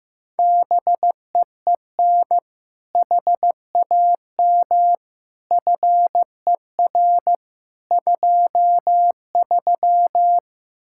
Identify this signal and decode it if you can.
Morse code, heard as a keyed tone, about 15 wpm, 700 Hz. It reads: BEEN HAM FER 23
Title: Practice QSO - Exchange 3 - More Info - 15wpm